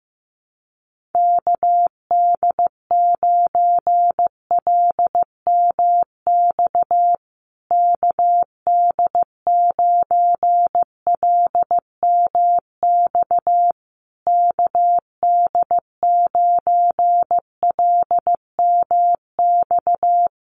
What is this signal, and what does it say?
Morse code, heard as a keyed tone, about 15 wpm, 700 Hz. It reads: KD9LMX KD9LMX KD9LMX
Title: USA Call Signs Repeated 3x 15wpm